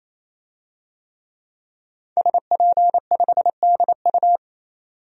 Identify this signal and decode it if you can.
Morse code, heard as a keyed tone, about 28 wpm, 700 Hz. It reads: SP5DU
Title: DXCC-100 Call Signs - 5 Characters - 28wpm